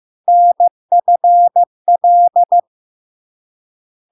Morse code, heard as a keyed tone, about 15 wpm, 700 Hz. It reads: NFL
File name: LARK-FD-2022-Log-in-CW-15wpm-700-Hz